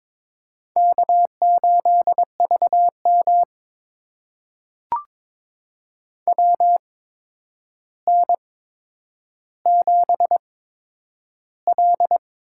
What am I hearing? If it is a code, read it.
Morse code, heard as a keyed tone, about 22 wpm, spaced as for 9 wpm, 700 Hz. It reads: K8VM WN7L
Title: USA Call Signs - 4 Characters - 8x Character Spacing - 22wpm